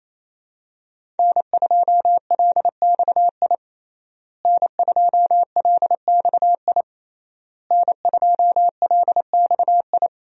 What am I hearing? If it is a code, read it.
Morse code, heard as a keyed tone, about 28 wpm, 700 Hz. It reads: N2LXI N2LXI N2LXI
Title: USA Call Signs - 5 Characters - Repeated 3x - 28wpm